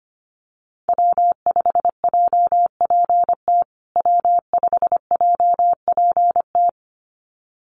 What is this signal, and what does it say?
Morse code, heard as a keyed tone, about 25 wpm, 700 Hz. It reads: W5JPT W5JPT
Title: USA Call Signs Repeated 2x 25wpm